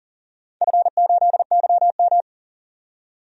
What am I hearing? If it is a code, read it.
Morse code, heard as a keyed tone, about 37 wpm, 700 Hz. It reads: F8YM
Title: DXCC-100 Call Signs - 4 Characters - 40wpm